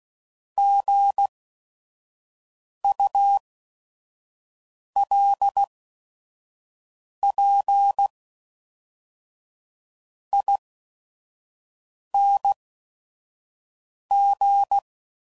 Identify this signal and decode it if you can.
Morse code, heard as a keyed tone, about 16 wpm, spaced as for 5 wpm, 800 Hz. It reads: GULPING
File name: Prefixes / By First Letter / Gs